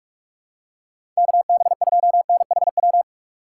Morse code, heard as a keyed tone, about 41 wpm, 700 Hz. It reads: KB2NHW
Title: USA Call Signs - 6 Characters - 6x Character Spacing - 45wpm